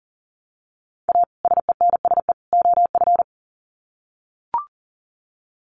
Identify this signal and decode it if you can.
Morse code, heard as a keyed tone, about 40 wpm, 700 Hz. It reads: A SENSE OF
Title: Sets of 3 Words 40wpm